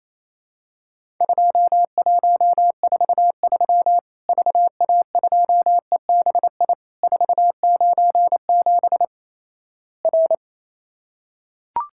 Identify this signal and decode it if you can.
Morse code, heard as a keyed tone, about 27 wpm, 700 Hz. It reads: 2143 VA2EBI 497 R
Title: Worked All Europe DX Contest - QTC Exchange - 28wpm